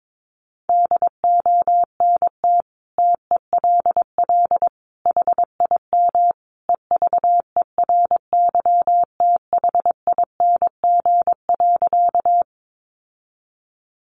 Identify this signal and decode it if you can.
Morse code, heard as a keyed tone, about 22 wpm, 700 Hz. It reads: DONT TELL HIM EVERYTHING.
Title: Sentences from Top 700 Words 22wpm